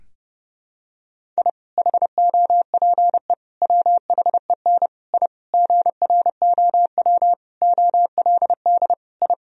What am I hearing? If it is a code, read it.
Morse code, heard as a keyed tone, about 32 wpm, 700 Hz. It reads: I HOPE WHEN I GROW OLD I
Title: Sentences from Top 1000 Words - Review Part 1 - 30wpm